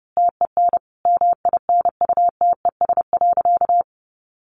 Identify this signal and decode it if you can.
Morse code, heard as a keyed tone, about 30 wpm, 700 Hz. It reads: TEN MINUTES.
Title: A Sound Of Thunder 30wpm